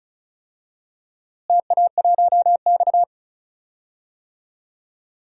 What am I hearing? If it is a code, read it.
Morse code, heard as a keyed tone, about 34 wpm, 700 Hz. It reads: TA1X
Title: DXCC-100 Call Signs - 4 Characters - Repeated 2x - 35wpm